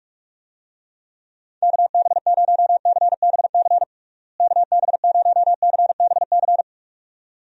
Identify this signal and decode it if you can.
Morse code, heard as a keyed tone, about 46 wpm, 700 Hz. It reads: KB0CBC KB0CBC
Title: USA Call Signs - 6 Characters - Repeated 2x - 45wpm